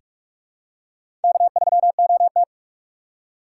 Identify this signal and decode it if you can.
Morse code, heard as a keyed tone, about 43 wpm, 700 Hz. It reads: K3OT
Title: USA Call Signs - 4 Characters - 2x Character Spacing - 45wpm